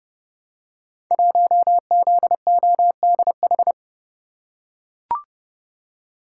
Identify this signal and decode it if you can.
Morse code, heard as a keyed tone, about 30 wpm, 700 Hz. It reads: 1ZODH
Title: Code Groups - 5 Letter:Numbers - 30wpm